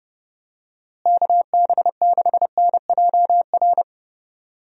Morse code, heard as a keyed tone, about 30 wpm, 700 Hz. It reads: KB6NJR
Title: USA Call Signs - 6 Characters - Repeated 3x - 30wpm